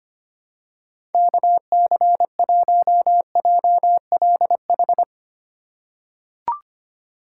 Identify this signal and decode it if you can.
Morse code, heard as a keyed tone, about 25 wpm, 700 Hz. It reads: KC1JLH
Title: USA Call Signs - 6 Characters - 6x Character Spacing - 25wpm